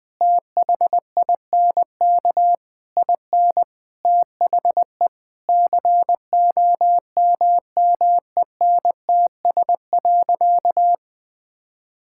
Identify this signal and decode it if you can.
Morse code, heard as a keyed tone, about 20 wpm, 700 Hz. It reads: THINK IN THE COMMENTS.
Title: Sentences with 8-Word Phrases - Spoken Beginning - 20wpm